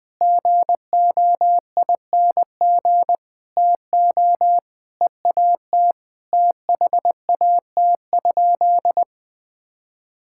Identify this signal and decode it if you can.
Morse code, heard as a keyed tone, about 21 wpm, 700 Hz. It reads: GOING TO EAT THAT?
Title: Sentences with 4-Word Phrases - 20wpm